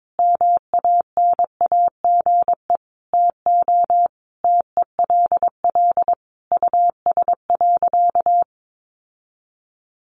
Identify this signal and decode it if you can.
Morse code, heard as a keyed tone, about 22 wpm, 700 Hz. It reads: MANAGE TO TELL US.
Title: Sentences from Top 1400 Words 22wpm